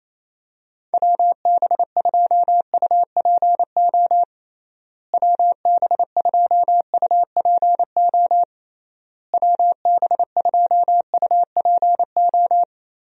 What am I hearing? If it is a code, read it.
Morse code, heard as a keyed tone, about 27 wpm, 700 Hz. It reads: WB2UPO WB2UPO WB2UPO
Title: USA Call Signs - 6 Characters - Repeated 3x - 28wpm